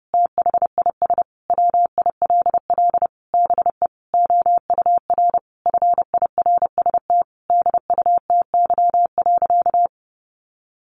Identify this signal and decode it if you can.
Morse code, heard as a keyed tone, about 30 wpm, 700 Hz. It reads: THIS WILL BE OUR FIRST DUTY.
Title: Sentences from Top 1500 Words 30wpm